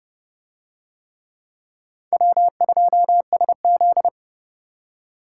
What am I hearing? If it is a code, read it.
Morse code, heard as a keyed tone, about 26 wpm, 700 Hz. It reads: W2SZ